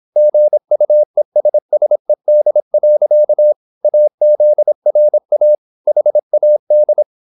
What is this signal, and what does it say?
Morse code, heard as a keyed tone, about 26 wpm, 600 Hz. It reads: GUESSED. AZRA HAD
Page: Never 26wpm | CW med Gnister